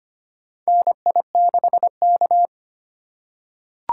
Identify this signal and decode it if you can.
Morse code, heard as a keyed tone, about 25 wpm, 700 Hz. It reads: NI6K
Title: USA Call Signs - 4 Characters - 8x Character Spacing - 25wpm